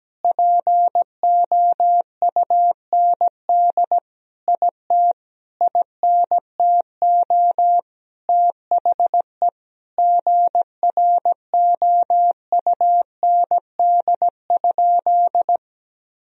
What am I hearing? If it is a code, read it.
Morse code, heard as a keyed tone, about 17 wpm, 700 Hz. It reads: POUND IT INTO THE GROUND?
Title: Sentences from Top 1000 Words - Review Part 2 - 17wpm